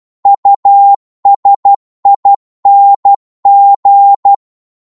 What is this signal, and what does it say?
Morse code, heard as a keyed tone, about 12 wpm, 800 Hz. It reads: USING
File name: Morse / qso-12wpm